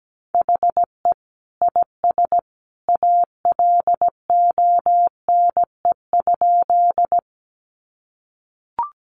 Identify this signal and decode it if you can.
Morse code, heard as a keyed tone, about 17 wpm, 700 Hz. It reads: HE IS ALONE?
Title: Sentences from Top 1000 Words 17wpm